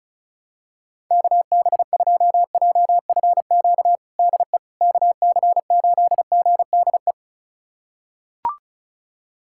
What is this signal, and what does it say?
Morse code, heard as a keyed tone, about 36 wpm, 700 Hz. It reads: KB2JFQ DE KC8GDE
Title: USA Callsign DE Callsign - 6 Characters - 35wpm